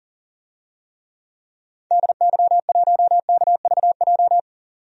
Morse code, heard as a keyed tone, about 40 wpm, 700 Hz. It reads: DY1KVJ
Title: DXCC-100 Call Signs - 6 Characters - 40wpm